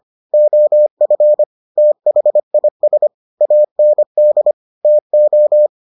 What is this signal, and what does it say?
Morse code, heard as a keyed tone, about 25 wpm, 600 Hz. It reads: OF THIS AND TO